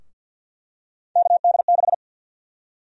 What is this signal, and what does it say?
Morse code, heard as a keyed tone, about 52 wpm, 700 Hz. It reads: KD6
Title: USA Call Sign Prefixes - 50wpm